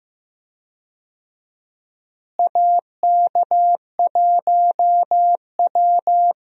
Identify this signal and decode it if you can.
Morse code, heard as a keyed tone, about 15 wpm, 700 Hz. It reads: AK1W
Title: USA Call Signs - 4 Characters - Repeated 3x - 15wpm